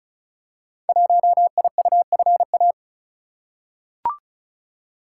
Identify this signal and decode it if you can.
Morse code, heard as a keyed tone, about 36 wpm, 700 Hz. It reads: 1IUFA
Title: Code Groups - 5 Letter:Numbers - 35wpm